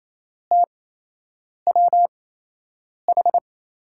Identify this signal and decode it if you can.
Morse code, heard as a keyed tone, about 27 wpm, spaced as for 8 wpm, 700 Hz. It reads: TWH
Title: USA Call Signs - 5 Characters - 8x Character Spacing - 28wpm